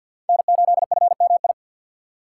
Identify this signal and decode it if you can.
Morse code, heard as a keyed tone, about 52 wpm, 700 Hz. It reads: N8FMI
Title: USA Call Signs - 5 Characters - 2x Character Spacing - 50wpm